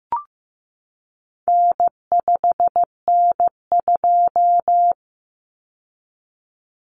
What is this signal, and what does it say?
Morse code, heard as a keyed tone, about 15 wpm, 700 Hz. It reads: N5N2
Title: Common Cut Numbers - 4-Digit 15wpm